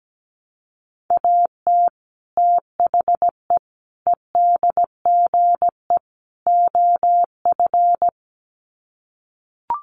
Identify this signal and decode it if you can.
Morse code, heard as a keyed tone, about 17 wpm, 700 Hz. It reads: AT THE EDGE OF
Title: Sets of 4 Words 17wpm